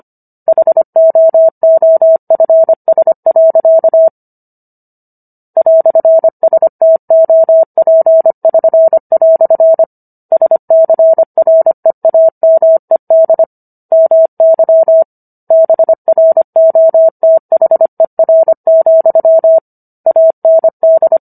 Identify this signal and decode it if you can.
Morse code, heard as a keyed tone, about 25 wpm, 650 Hz. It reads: HOOFS. "STOP<SN>" SCREAMED MY BROTHER, AND